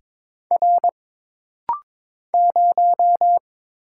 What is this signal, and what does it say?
Morse code, heard as a keyed tone, about 22 wpm, 700 Hz. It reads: R 0
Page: This rapid-fire timing is geared to push your ICR (Instant Character Recognition) skills.